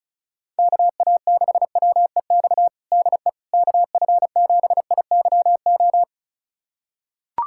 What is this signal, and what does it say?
Morse code, heard as a keyed tone, about 34 wpm, 700 Hz. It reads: KA6WEX DE KF7IYO
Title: USA Callsign DE Callsign - 6 Characters - 35wpm